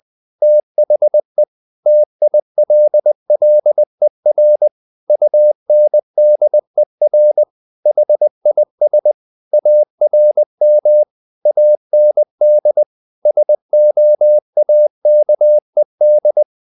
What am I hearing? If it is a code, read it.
Morse code, heard as a keyed tone, about 20 wpm, 600 Hz. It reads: THE TILLER UNDER HIS ARM AND SOAKED
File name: the old man and the sea part 2 20wpm